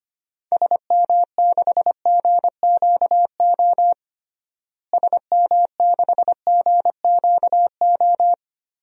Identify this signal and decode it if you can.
Morse code, heard as a keyed tone, about 25 wpm, 700 Hz. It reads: SM6GQO SM6GQO
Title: DXCC-100 Call Signs - 6 Characters - Repeated 2x - 25wpm